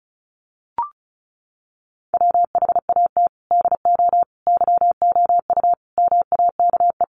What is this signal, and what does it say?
Morse code, heard as a keyed tone, about 34 wpm, 700 Hz. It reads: WHAT DO YOU MAKE
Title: Sets of 5 Words 35wpm